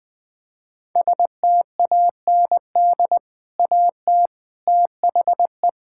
Morse code, heard as a keyed tone, about 20 wpm, 700 Hz. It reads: STAND AT THE
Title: Sentences from Top 1600 Words 20wpm